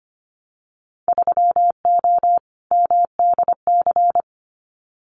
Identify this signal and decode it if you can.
Morse code, heard as a keyed tone, about 25 wpm, 700 Hz. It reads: 3O MDC
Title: Winter Field Day Exchanges Encore 25wpm